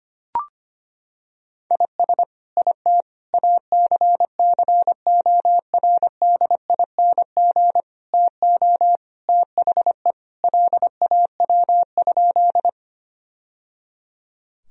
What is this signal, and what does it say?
Morse code, heard as a keyed tone, about 25 wpm, 700 Hz. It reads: IS IT ACCORDING TO THE LAW?
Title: Sentences from Top 1000 Words - Review Part 4 - 25wpm